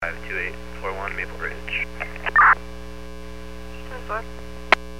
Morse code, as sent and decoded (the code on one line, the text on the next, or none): none